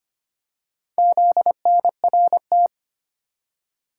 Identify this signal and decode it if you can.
Morse code, heard as a keyed tone, about 25 wpm, 700 Hz. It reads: ZNRT
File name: practice / 25wpm / Four Letter-Number Combinations - 25wpm